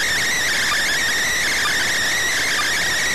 Emergency Alert System (SAME) tones, heard as harsh buzzing bursts. It is heard from 0.2 to 2.6 s.